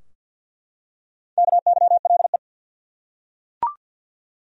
Morse code, heard as a keyed tone, about 51 wpm, 700 Hz. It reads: KYLE
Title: USA Common Names - Max 4 Letters - 50wpm